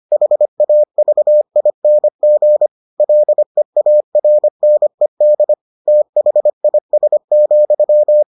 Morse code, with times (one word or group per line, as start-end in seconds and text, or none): 0.1-2.7 HAVING
3.0-5.5 LEARNED
5.9-8.2 THIS,